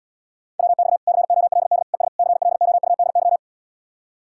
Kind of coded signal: Morse code

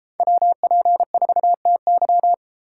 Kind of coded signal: Morse code

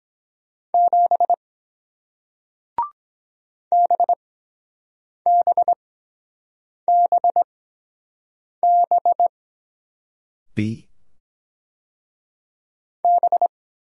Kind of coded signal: Morse code

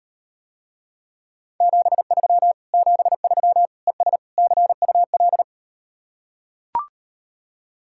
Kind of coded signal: Morse code

